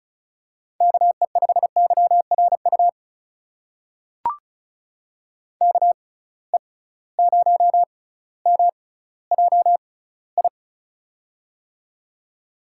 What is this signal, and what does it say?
Morse code, heard as a keyed tone, about 34 wpm, spaced as for 18 wpm, 700 Hz. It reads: KE5YRU KE0MJI
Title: USA Call Signs - 6 Characters - 6x Character Spacing - 35wpm